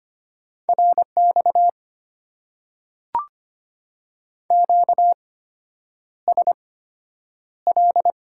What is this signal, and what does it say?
Morse code, heard as a keyed tone, about 25 wpm, spaced as for 9 wpm, 700 Hz. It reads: RX QSL
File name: Top 90 QSO Elements - 8x Character Spacing - 25wpm